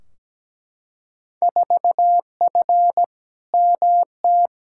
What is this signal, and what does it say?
Morse code, heard as a keyed tone, about 17 wpm, 700 Hz. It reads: 4F MT
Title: Field Day Exchanges Encore - 17wpm